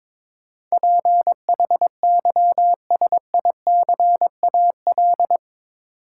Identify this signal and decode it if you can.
Morse code, heard as a keyed tone, about 22 wpm, 700 Hz. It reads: PHYSICAL
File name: practice / 22wpm / Eight Letter Words - 22wpm